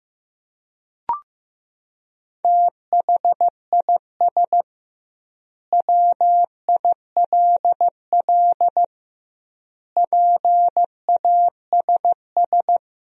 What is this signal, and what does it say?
Morse code, heard as a keyed tone, about 15 wpm, 700 Hz. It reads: THIS WILL PASS
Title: 2x Word-Spacing - Sentences from Top 400 Words - 15wpm